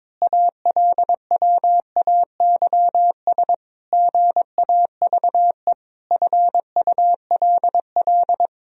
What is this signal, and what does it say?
Morse code, heard as a keyed tone, about 22 wpm, 700 Hz. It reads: ALWAYS GAVE FULL
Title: Sentences from Top 1000 Words - Review Part 3 - 22wpm